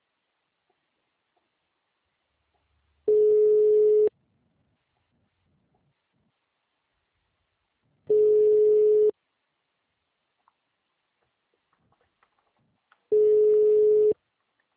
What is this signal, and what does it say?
Ringback tone (Europe, 425 Hz).
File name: out- -2005- - - .4106